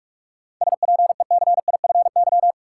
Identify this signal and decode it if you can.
Morse code, heard as a keyed tone, about 46 wpm, 700 Hz. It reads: SPECIFY
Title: Seven Letter Words - 45wpm